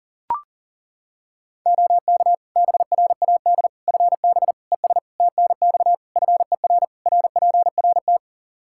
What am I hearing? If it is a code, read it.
Morse code, heard as a keyed tone, about 40 wpm, 700 Hz. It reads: OK BRAD FB ES TNX FER RPRT
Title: K1USN SST QSO Practice 40wpm